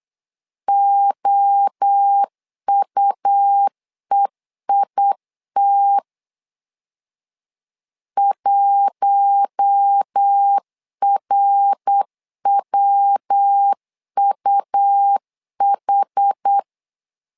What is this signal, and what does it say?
Morse code, heard as a keyed tone, about 8 wpm, 800 Hz. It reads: OUEIT 1RWUH
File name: LEZIONI / 4 Q1 8wpm 3035